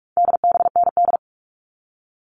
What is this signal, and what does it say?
Morse code, heard as a keyed tone, about 44 wpm, 700 Hz. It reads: DBND
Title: All Cut Numbers - 4-Digit 45wpm